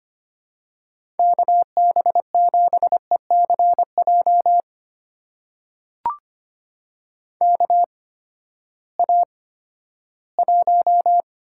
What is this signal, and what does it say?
Morse code, heard as a keyed tone, about 25 wpm, spaced as for 14 wpm, 700 Hz. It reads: KB7ECJ KA1
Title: USA Call Signs - 6 Characters - 8x Character Spacing - 25wpm